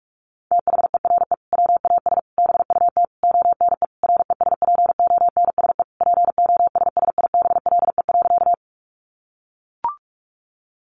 Morse code, heard as a keyed tone, about 44 wpm, 700 Hz. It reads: THERE WAS BUT ONE RESPONSE POSSIBLE.
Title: Sentences from Top 1400 Words 45wpm